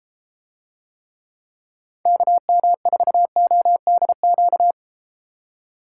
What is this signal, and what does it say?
Morse code, heard as a keyed tone, about 33 wpm, 700 Hz. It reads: KM4ODQ